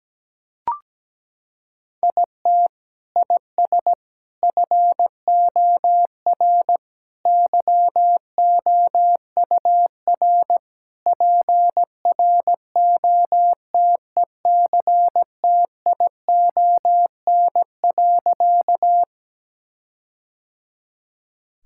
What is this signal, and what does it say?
Morse code, heard as a keyed tone, about 17 wpm, 700 Hz. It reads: IT IS FOR YOUR PROTECTION.
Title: Sentences from Top 1600 Words 17wpm